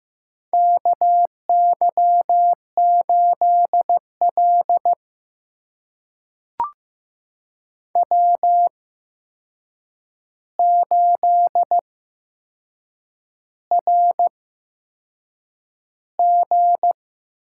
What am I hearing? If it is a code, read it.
Morse code, heard as a keyed tone, about 15 wpm, spaced as for 6 wpm, 700 Hz. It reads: KY8L W8RG
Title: USA Call Signs - 4 Characters - 8x Character Spacing - 15wpm